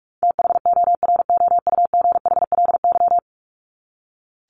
Morse code, heard as a keyed tone, about 44 wpm, 700 Hz. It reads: THOROUGHLY
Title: LY Suffix - Top 100 Words with Base Words 45wpm